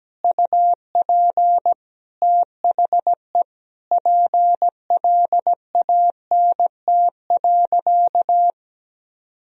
Morse code, heard as a keyed tone, about 17 wpm, 700 Hz. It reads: UP THE PLANT.
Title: Sentences from Top 1000 Words - Review Part 1 - 17wpm